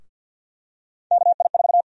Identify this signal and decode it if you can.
Morse code, heard as a keyed tone, about 50 wpm, 700 Hz. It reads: KI4
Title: USA Call Sign Prefixes - 50wpm